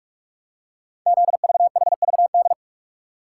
Morse code, heard as a keyed tone, about 44 wpm, 700 Hz. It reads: ZVHVD
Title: Code Groups - 5 Letters - 45wpm